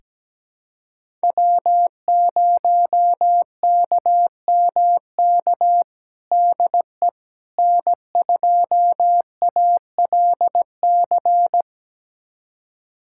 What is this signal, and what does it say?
Morse code, heard as a keyed tone, about 17 wpm, 700 Hz. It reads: W0KMK DE N2ALC
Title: USA Callsign DE Callsign - 5 Characters - 17wpm